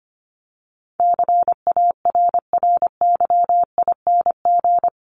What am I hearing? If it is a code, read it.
Morse code, heard as a keyed tone, about 25 wpm, 700 Hz. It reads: CARRYING
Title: ING Suffix - Top 100 Words with Base Words 25wpm